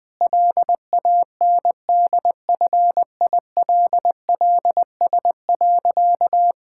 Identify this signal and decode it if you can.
Morse code, heard as a keyed tone, about 20 wpm, 700 Hz. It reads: LANDFILLS.
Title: Difficult Sentences with Hints - 20wpm